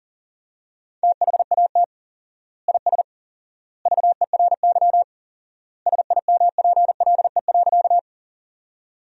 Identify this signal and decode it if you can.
Morse code, heard as a keyed tone, about 40 wpm, 700 Hz. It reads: THAT IS VERY SIMPLE.
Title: 4x Word-Spacing - Sentences from Top 400 Words - 40wpm